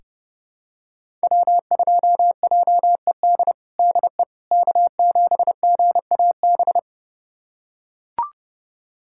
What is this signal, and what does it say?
Morse code, heard as a keyed tone, about 32 wpm, 700 Hz. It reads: W2JED DE K7GAB
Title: USA Callsign DE Callsign - 5 Characters - 30wpm